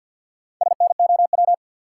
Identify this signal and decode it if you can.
Morse code, heard as a keyed tone, about 50 wpm, 700 Hz. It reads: SNOW
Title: Sentences from Top 500 Words - New Word Drill - 50wpm